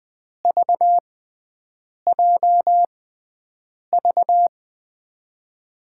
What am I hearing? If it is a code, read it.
Morse code, heard as a keyed tone, about 21 wpm, spaced as for 7 wpm, 700 Hz. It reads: VJV
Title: USA Call Signs - 5 Characters - 6x Character Spacing - 20wpm